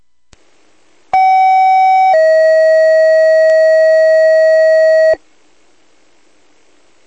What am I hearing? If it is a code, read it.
Two-tone page, heard as a one-second tone, then a three-second tone, about 760 then 640 Hz. It is heard from 1.1 to 5.1 s.
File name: old-forge